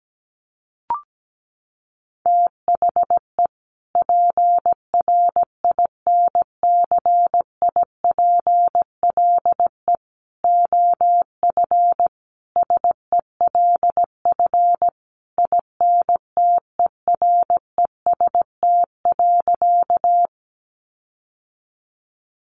Morse code, heard as a keyed tone, about 17 wpm, 700 Hz. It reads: THE PRINCIPLE OF SELF INTEREST.
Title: Sentences from Top 1400 Words 17wpm